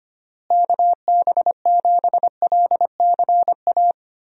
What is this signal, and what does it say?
Morse code, heard as a keyed tone, about 25 wpm, 700 Hz. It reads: KB7LCA
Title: USA Call Signs - 6 Characters - Repeated 2x - 25wpm